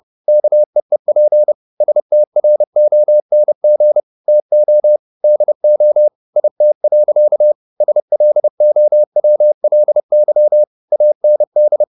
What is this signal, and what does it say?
Morse code, heard as a keyed tone, about 30 wpm, 600 Hz. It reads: KEEP STRONG TO DO IT. SLOWLY AND
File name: the old man and the sea / the old man and the sea 1 30wpm